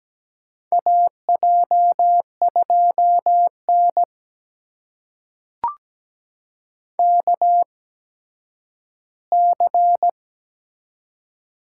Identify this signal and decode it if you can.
Morse code, heard as a keyed tone, about 17 wpm, spaced as for 10 wpm, 700 Hz. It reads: AJ2N KC
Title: USA Call Signs - 8x Character Spacing - 17wpm